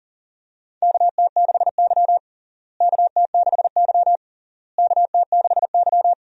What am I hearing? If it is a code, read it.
Morse code, heard as a keyed tone, about 40 wpm, 700 Hz. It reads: KT6Y KT6Y KT6Y
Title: USA Call Signs - 4 Characters - Repeated 3x - 40wpm